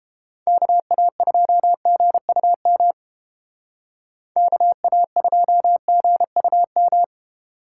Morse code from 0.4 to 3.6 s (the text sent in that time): KA2GUM